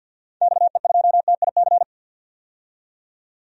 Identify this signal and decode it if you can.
Morse code, heard as a keyed tone, about 53 wpm, 700 Hz. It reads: XE2TIC
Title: DXCC-100 Call Signs - 6 Characters - 50wpm